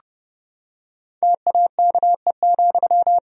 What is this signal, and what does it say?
Morse code, heard as a keyed tone, about 32 wpm, 700 Hz. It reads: TAKE,
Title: Sentences from Top 1000 Words - Review Part 1 - 30wpm